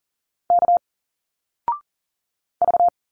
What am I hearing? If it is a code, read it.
Morse code, heard as a keyed tone, about 40 wpm, 700 Hz. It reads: K V
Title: Single Letters 40wpm